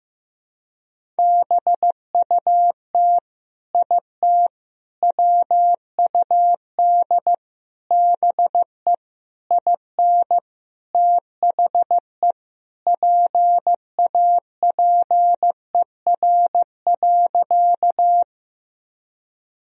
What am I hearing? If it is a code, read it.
Morse code, heard as a keyed tone, about 15 wpm, 700 Hz. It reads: BUT IT WUD BE IN THE PAPER.
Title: Sentences from Top 300 Words and Abbreviations 15wpm